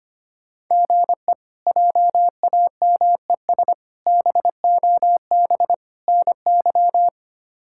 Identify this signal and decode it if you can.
Morse code, heard as a keyed tone, about 25 wpm, 700 Hz. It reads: GE JAMES BOB NY
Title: K1USN SST QSO Practice - 25wpm